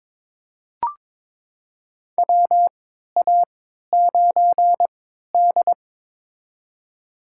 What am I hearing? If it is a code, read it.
Morse code, heard as a keyed tone, about 22 wpm, spaced as for 13 wpm, 700 Hz. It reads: WA9D
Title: USA Call Signs - 4 Characters - 3x Character Spacing - 22wpm